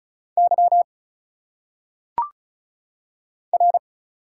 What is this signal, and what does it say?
Morse code, heard as a keyed tone, about 37 wpm, 700 Hz. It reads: Y R
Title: Single Letters - Phonetic Pronunciation 35wpm